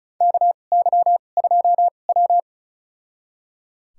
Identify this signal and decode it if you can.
Morse code, heard as a keyed tone, about 34 wpm, spaced as for 25 wpm, 700 Hz. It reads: KY2W
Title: USA Call Signs - 2x Character Spacing - 35wpm